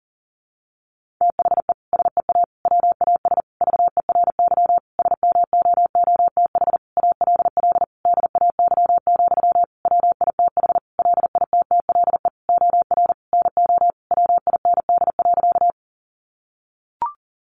Morse code, heard as a keyed tone, about 40 wpm, 700 Hz. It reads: THE SEA WAS VERY SMOOTH ALL DAY, WITH LITTLE OR NO WIND.
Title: Sentences from Top 1400 Words 40wpm